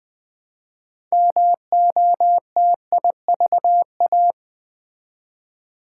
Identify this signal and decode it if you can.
Morse code, heard as a keyed tone, about 20 wpm, 700 Hz. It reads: MOTIVA
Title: Build A Word - 2 Letters At A Time - 8-Letter Words - 20wpm